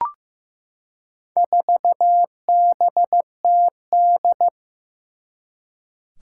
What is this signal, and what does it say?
Morse code, heard as a keyed tone, about 15 wpm, 700 Hz. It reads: 4BTD
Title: Four Challenging Letter-Number Combinations 15wpm